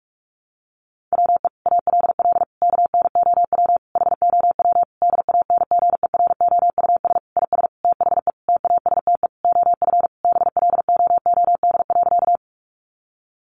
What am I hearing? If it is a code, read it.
Morse code, heard as a keyed tone, about 44 wpm, 700 Hz. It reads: WE ALL KNOW HOW DANGEROUS IS THE TASTE OF BLOOD.
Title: Sentences from Top 1200 Words 45wpm